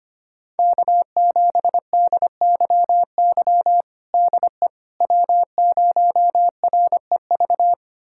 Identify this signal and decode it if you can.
Morse code, heard as a keyed tone, about 25 wpm, 700 Hz. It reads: K7DYY DE W0REV
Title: USA Callsign DE Callsign - 5 Characters - 25wpm